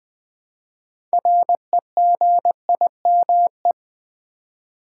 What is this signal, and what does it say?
Morse code, heard as a keyed tone, about 20 wpm, 700 Hz. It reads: REGIME
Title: Build A Word - 2 Letters At A Time - 6-Letter Words - 20wpm